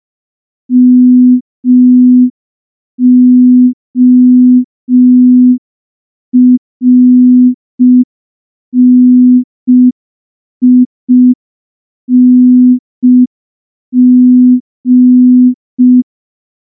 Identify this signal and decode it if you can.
Morse code, heard as a keyed tone, about 5 wpm, 250 Hz. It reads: MORNING
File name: morse / morse-training-05WPM